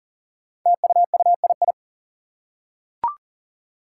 Morse code, heard as a keyed tone, about 38 wpm, 700 Hz. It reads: TUUII
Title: Five Challenging Letter-Number Combinations 40wpm